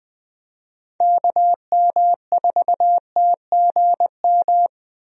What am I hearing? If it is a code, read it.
Morse code, heard as a keyed tone, about 21 wpm, 700 Hz. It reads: KM4TGM
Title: USA Call Signs - 6 Characters - Repeated 2x - 20wpm